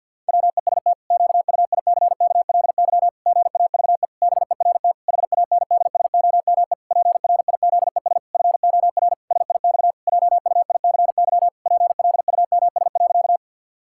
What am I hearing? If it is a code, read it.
Morse code, heard as a keyed tone, about 50 wpm, 700 Hz. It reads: WEST QUICKLY GAVE BERT HANDSOME PRIZES FOR SIX JUICY PLUMS.
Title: Pangram Challenge v2 - 50wpm